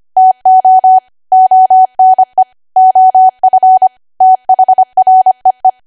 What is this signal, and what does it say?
Morse code, heard as a keyed tone, about 25 wpm, 750 Hz. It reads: TO ONE OF THREE
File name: cwa / W1AW-Level3 / 25WPM